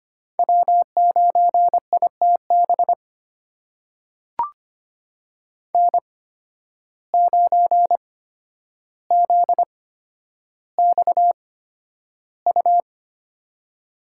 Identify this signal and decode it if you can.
Morse code, heard as a keyed tone, about 25 wpm, spaced as for 11 wpm, 700 Hz. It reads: W9ITB N9ZXU
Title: USA Call Signs - 8x Character Spacing - 25wpm